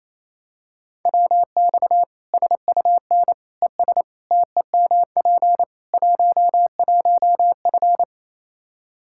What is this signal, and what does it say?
Morse code, heard as a keyed tone, about 28 wpm, 700 Hz. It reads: WX SUN ES TEMP 11F
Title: Practice QSO - Exchange 2 - Info - 28wpm